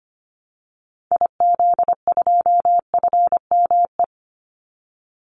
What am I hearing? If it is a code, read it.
Morse code, heard as a keyed tone, about 25 wpm, 700 Hz. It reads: IZ2FME
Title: Call Signs Repeated 3x - 25wpm